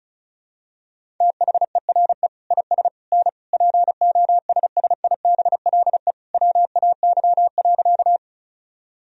Morse code, heard as a keyed tone, about 34 wpm, 700 Hz. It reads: THERE IS N POSSIBLE WAY.
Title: Sentences from Top 500 Words and Abbreviations 35wpm